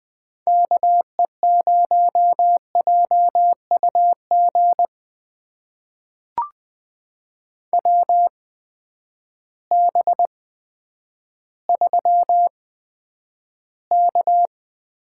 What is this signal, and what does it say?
Morse code, heard as a keyed tone, about 20 wpm, spaced as for 10 wpm, 700 Hz. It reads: KE0JUG WB3K
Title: USA Call Signs - 6 Characters - 8x Character Spacing - 20wpm